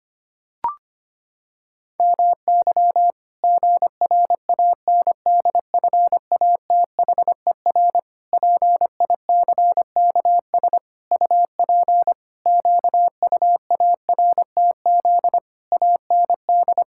Morse code, heard as a keyed tone, about 25 wpm, 700 Hz. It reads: MY GRANDFATHER PICKS UP QUARTZ AND
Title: Pangram Challenge v2 - 25wpm